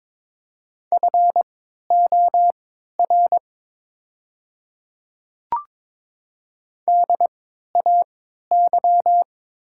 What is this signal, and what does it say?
Morse code, heard as a keyed tone, about 22 wpm, spaced as for 12 wpm, 700 Hz. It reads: FOR DAY
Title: 3x Character Spacing - Top 200 Words - 22wpm